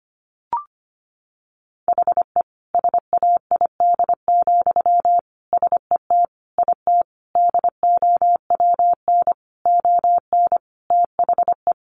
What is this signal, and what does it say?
Morse code, heard as a keyed tone, about 25 wpm, 700 Hz. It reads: HE SAID, SET IT DOWN ON THE
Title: Sentences from Top 1400 Words 25wpm